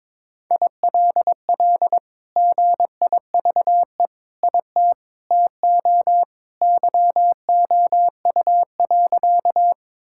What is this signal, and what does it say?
Morse code, heard as a keyed tone, about 22 wpm, 700 Hz. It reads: ILL GIVE IT TO YOU.
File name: practice / 22wpm / Sentences with 2-Word Phrases - 22wpm